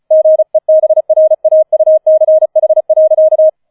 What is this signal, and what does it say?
Morse code, heard as a keyed tone, about 35 wpm, 630 Hz. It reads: GEBRAUCH.